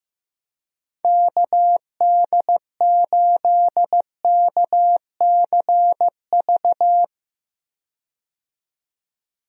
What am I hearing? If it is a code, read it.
Morse code, heard as a keyed tone, about 15 wpm, 700 Hz. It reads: KD8KCV
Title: USA Call Signs - 6 Characters - 15wpm